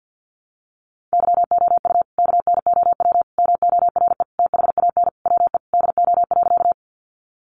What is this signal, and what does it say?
Morse code, heard as a keyed tone, about 50 wpm, 700 Hz. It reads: YOU KNOW MORE THAN WE DO.
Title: Sentences from Top 100 Words 50wpm